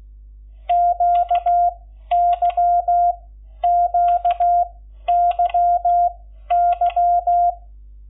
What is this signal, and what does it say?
Morse code, heard as a keyed tone, about 16 wpm, spaced as for 11 wpm, 700 Hz. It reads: QYQYY
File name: Lecciones / CLASE Nº11